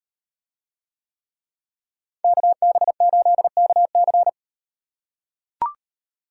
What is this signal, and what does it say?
Morse code, heard as a keyed tone, about 38 wpm, 700 Hz. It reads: KB8KC